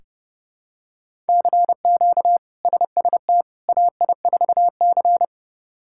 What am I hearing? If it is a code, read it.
Morse code, heard as a keyed tone, about 30 wpm, 700 Hz. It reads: CQ SST AI4C
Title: K1USN SST QSO Practice - 30wpm